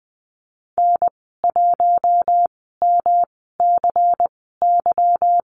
Morse code, heard as a keyed tone, about 19 wpm, spaced as for 14 wpm, 700 Hz. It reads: N1MCY
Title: 147 Lesson 24g - Letter y - Call Signs 20wpm